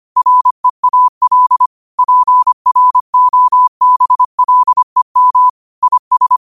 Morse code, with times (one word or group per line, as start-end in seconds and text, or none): 0.2-1.7 REAL
2.0-5.5 PROBLEM
5.8-6.4 IS